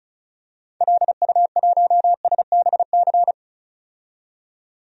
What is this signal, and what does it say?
Morse code, heard as a keyed tone, about 34 wpm, 700 Hz. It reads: LU1SBC
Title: DXCC-100 Call Signs - 6 Characters - 35wpm